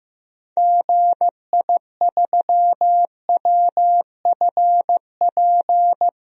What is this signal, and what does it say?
Morse code, heard as a keyed tone, about 15 wpm, 700 Hz. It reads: GI3WFP
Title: DXCC-100 Call Signs - 6 Characters - 15wpm